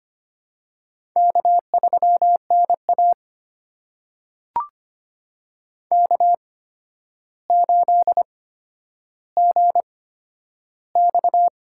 Morse code, heard as a keyed tone, about 25 wpm, spaced as for 11 wpm, 700 Hz. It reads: K3NA K8GX
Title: USA Call Signs - 8x Character Spacing - 25wpm